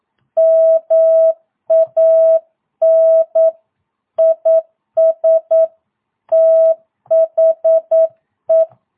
Morse code, heard as a keyed tone, about 9 wpm, 650 Hz. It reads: MAN IS THE
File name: jh8cbh / 2022cw